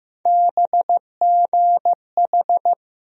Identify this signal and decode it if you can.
Morse code, heard as a keyed tone, about 15 wpm, 700 Hz. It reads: BGH
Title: USA Call Sign Postfixes - 3 Letters 15wpm